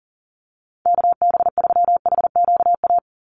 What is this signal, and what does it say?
Morse code, heard as a keyed tone, about 40 wpm, 700 Hz. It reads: KB3HQA
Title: USA Call Signs 40wpm